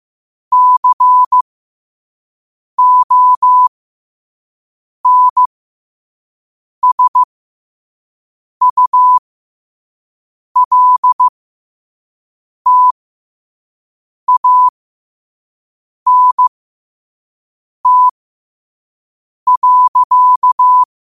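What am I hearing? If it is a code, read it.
Morse code, heard as a keyed tone, about 15 wpm, spaced as for 5 wpm, 1000 Hz. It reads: CONSULTANT.
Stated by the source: Morse code quote of the day sent at 5 wpm.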